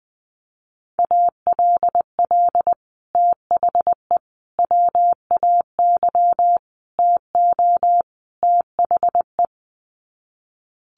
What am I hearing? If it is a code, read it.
Morse code, heard as a keyed tone, about 20 wpm, 700 Hz. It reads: ALL THE WAY TO THE
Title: Sets of 5 Words 20wpm